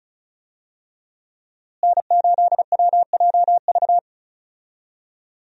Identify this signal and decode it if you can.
Morse code, heard as a keyed tone, about 36 wpm, 700 Hz. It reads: N8WJV
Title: USA Call Signs - 5 Characters - Repeated 3x - 35wpm